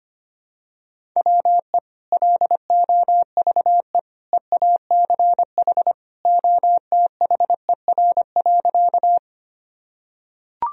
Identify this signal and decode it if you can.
Morse code, heard as a keyed tone, about 25 wpm, 700 Hz. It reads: WE LOVE EACH OTHER.
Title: Sentences from Top 1000 Words - Review Part 1 - 25wpm